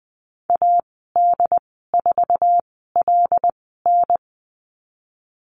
Morse code, heard as a keyed tone, about 20 wpm, spaced as for 14 wpm, 700 Hz. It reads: AD4LN
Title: 076 Lesson 13g - Letter L - Call Signs 20wpm